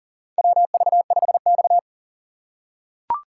Morse code, heard as a keyed tone, about 37 wpm, 700 Hz. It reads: WV5X
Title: USA Call Signs - 8x Character Spacing - 40wpm